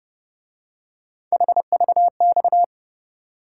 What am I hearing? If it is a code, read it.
Morse code, heard as a keyed tone, about 29 wpm, 700 Hz. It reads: HVX
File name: Speed-Racing - Three Letter-Number Combinations - 25wpm